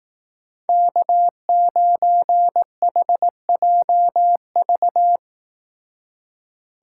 Morse code, heard as a keyed tone, about 18 wpm, 700 Hz. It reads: K9HJV
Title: Speed Racing - USA 1x3 Call Signs 15wpm